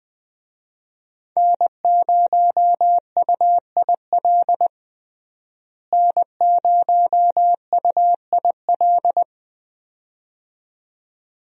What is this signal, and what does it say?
Morse code, heard as a keyed tone, about 20 wpm, 700 Hz. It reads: N0UIL N0UIL
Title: USA Call Signs - 5 Characters - Repeated 2x - 20wpm